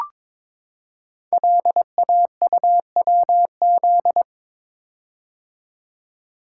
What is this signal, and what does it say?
Morse code, heard as a keyed tone, about 22 wpm, 690 Hz. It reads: LAUWZ
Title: Code Groups - 5 Letter:Numbers - 22wpm